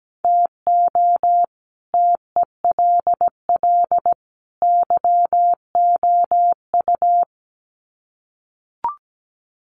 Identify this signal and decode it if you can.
Morse code, heard as a keyed tone, about 17 wpm, 700 Hz. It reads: TO TELL YOU
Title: Sets of 3 Words 17wpm